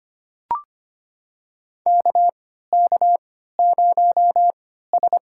Morse code, heard as a keyed tone, about 25 wpm, spaced as for 14 wpm, 700 Hz. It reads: KK0S
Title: USA Call Signs - 4 Characters - 3x Character Spacing - 25wpm